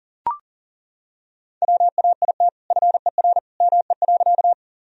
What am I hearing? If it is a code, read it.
Morse code, heard as a keyed tone, about 40 wpm, 700 Hz. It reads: WAIT FER ME.
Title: Sentences from Top 500 Words and Abbreviations 40wpm